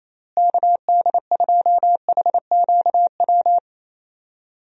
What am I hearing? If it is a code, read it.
Morse code, heard as a keyed tone, about 28 wpm, 700 Hz. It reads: KD2HQW
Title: USA Call Signs - 6 Characters - Repeated 2x - 28wpm